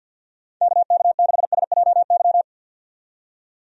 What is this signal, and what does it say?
Morse code, heard as a keyed tone, about 51 wpm, 700 Hz. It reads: KK6SJY
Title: USA Call Signs - 6 Characters - 50wpm